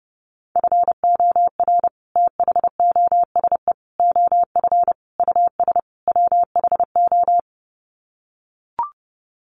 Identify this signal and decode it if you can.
Morse code, heard as a keyed tone, about 30 wpm, 700 Hz. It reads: FOR THOSE OF US WHO
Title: Sets of 5 Words Encore 30wpm